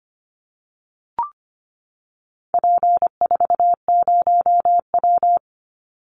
Morse code, heard as a keyed tone, about 25 wpm, 700 Hz. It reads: P40W
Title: Call Signs Repeated 2x - 25wpm